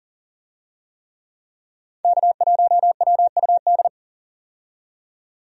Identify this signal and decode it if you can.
Morse code, heard as a keyed tone, about 37 wpm, 700 Hz. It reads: K1WUD
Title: USA Call Signs - 5 Characters - 40wpm